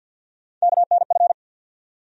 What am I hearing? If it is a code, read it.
Morse code, heard as a keyed tone, about 52 wpm, 700 Hz. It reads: KNF
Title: USA Call Sign Postfixes - 3 Letters 50wpm